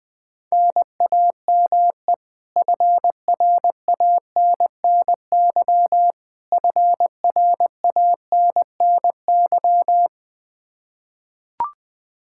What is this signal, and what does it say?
Morse code, heard as a keyed tone, about 20 wpm, 700 Hz. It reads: NAME FRANNY FRANNY
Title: Practice QSO - Exchange 1 - Beginning - 20wpm